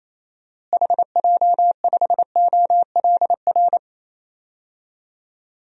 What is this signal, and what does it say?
Morse code, heard as a keyed tone, about 28 wpm, 700 Hz. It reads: HJ5OLR
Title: DXCC-100 Call Signs - 6 Characters - Repeated 3x - 28wpm